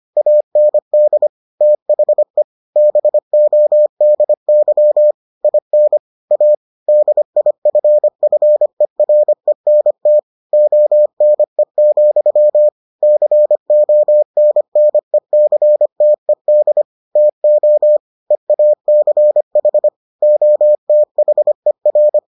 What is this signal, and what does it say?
Morse code, heard as a keyed tone, about 25 wpm, 600 Hz. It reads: AND THE BODY IN A DIFFERENT ONE, CONNECTED TO EACH OTHER